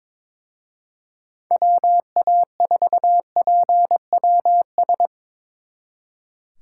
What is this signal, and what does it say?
Morse code, heard as a keyed tone, about 25 wpm, 700 Hz. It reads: WA4PWS
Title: Speed Racing - USA 2x3 Call Signs 22wpm